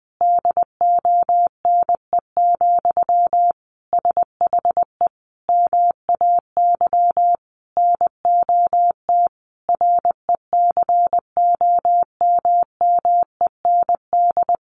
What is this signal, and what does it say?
Morse code, heard as a keyed tone, about 20 wpm, 700 Hz. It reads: DONE, SHE MAY NOT RECOMMEND
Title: Sentences from Top 1200 Words 20wpm